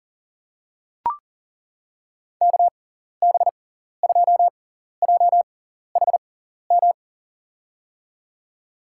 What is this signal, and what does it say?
Morse code, heard as a keyed tone, about 40 wpm, spaced as for 14 wpm, 700 Hz. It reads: KB2JHM
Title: USA Call Signs - 6 Characters - 6x Character Spacing - 40wpm